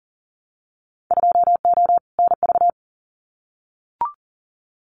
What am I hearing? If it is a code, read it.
Morse code, heard as a keyed tone, about 40 wpm, 700 Hz. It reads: 2O NV
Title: Winter Field Day Exchanges 40wpm